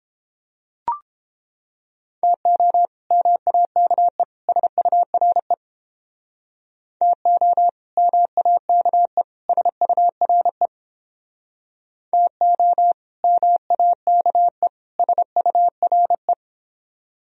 Morse code, heard as a keyed tone, about 29 wpm, 700 Hz. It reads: TO MAKE SURE TO MAKE SURE TO MAKE SURE